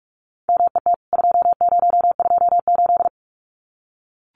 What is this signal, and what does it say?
Morse code, heard as a keyed tone, about 44 wpm, 700 Hz. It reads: MET 2028
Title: CWT Exchange 45wpm